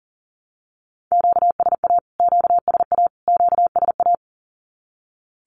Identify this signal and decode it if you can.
Morse code, heard as a keyed tone, about 40 wpm, 700 Hz. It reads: QSA QSA QSA
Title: Q-Codes Repeated 3x 40wpm